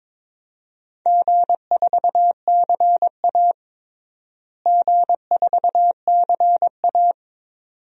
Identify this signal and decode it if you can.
Morse code, heard as a keyed tone, about 21 wpm, 700 Hz. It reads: G4CA G4CA
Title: DXCC-100 Call Signs - 4 Characters - Repeated 2x - 22wpm